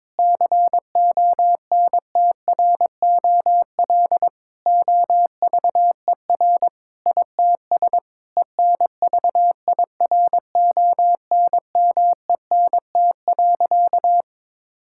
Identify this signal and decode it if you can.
Morse code, heard as a keyed tone, about 22 wpm, 700 Hz. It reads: CONTROL OVER ITS ENVIRONMENT.
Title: Sentences from Top 1000 Words - Review Part 5 - 22wpm